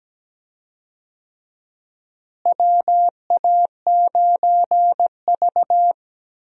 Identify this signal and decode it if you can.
Morse code, heard as a keyed tone, about 17 wpm, 700 Hz. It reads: WA9V
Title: USA Call Signs - 4 Characters - Repeated 2x - 17wpm